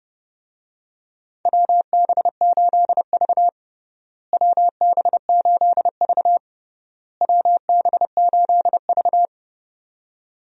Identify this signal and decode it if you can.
Morse code, heard as a keyed tone, about 30 wpm, 700 Hz. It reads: WB8V WB8V WB8V
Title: USA Call Signs - 4 Characters - Repeated 3x - 30wpm